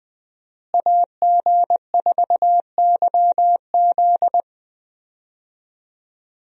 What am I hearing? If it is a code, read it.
Morse code, heard as a keyed tone, about 20 wpm, 700 Hz. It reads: AG4YZ
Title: USA Call Signs - 5 Characters - Repeated 2x - 20wpm